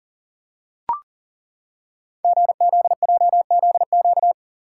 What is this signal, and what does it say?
Morse code, heard as a keyed tone, about 39 wpm, 690 Hz. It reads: GZJZQ
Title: Code Groups - 5 Letter:Numbers - 40wpm